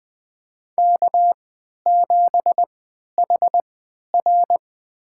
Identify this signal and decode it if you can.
Morse code, heard as a keyed tone, about 20 wpm, spaced as for 11 wpm, 700 Hz. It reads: K7HR
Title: USA Call Signs - 4 Characters - 3x Character Spacing - 20wpm